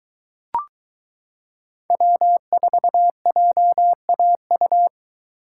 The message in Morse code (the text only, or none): W4JAU